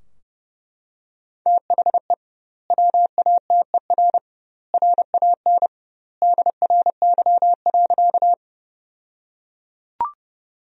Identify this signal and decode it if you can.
Morse code, heard as a keyed tone, about 32 wpm, 700 Hz. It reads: THE WATER RAN DRY.
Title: 2x Word-Spacing - Sentences from Top 500 Words - 30wpm